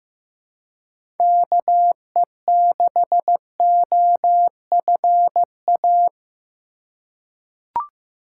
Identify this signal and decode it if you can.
Morse code, heard as a keyed tone, about 15 wpm, 700 Hz. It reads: KE6OFA
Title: USA Call Signs - 6 Characters - 3x Character Spacing - 15wpm